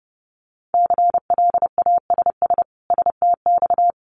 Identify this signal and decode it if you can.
Morse code, heard as a keyed tone, about 30 wpm, 700 Hz. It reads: CLASS STX
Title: Winter Field Day Exchanges 30wpm